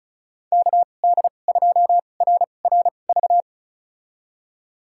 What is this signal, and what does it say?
Morse code, heard as a keyed tone, about 36 wpm, spaced as for 25 wpm, 700 Hz. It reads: KD2RRV
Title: USA Call Signs - 2x Character Spacing - 35wpm